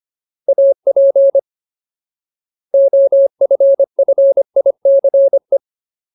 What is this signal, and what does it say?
Morse code, heard as a keyed tone, about 25 wpm, 550 Hz. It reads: AP OFFICE
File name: Ultimate Word List / 25WPM Ultimate